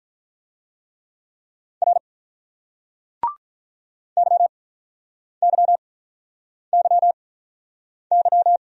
Morse code, heard as a keyed tone, about 48 wpm, spaced as for 9 wpm, 700 Hz. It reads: R YYYY